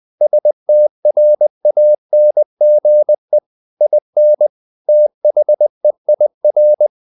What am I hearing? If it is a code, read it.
Morse code, heard as a keyed tone, about 20 wpm, 600 Hz. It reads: STRANGE IN THEIR
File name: the old man and the sea / the old man and the sea part 1 20wpm